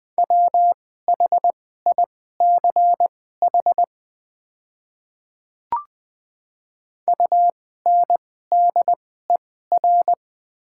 Morse code, heard as a keyed tone, about 20 wpm, spaced as for 14 wpm, 700 Hz. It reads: WHICH UNDER
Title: 2x Character Spacing - Top 300 Words - 20wpm